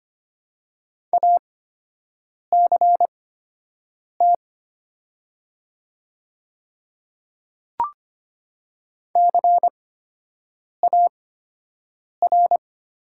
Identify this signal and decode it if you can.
Morse code, heard as a keyed tone, about 25 wpm, spaced as for 7 wpm, 690 Hz. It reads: ACT CAR
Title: 8x Character Spacing - Top 300 Words - 25wpm